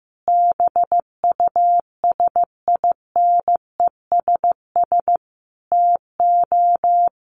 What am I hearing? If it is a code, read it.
Morse code, heard as a keyed tone, about 15 wpm, 700 Hz. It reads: BUSINESS TO
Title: Sentences from Top 1500 Words 15wpm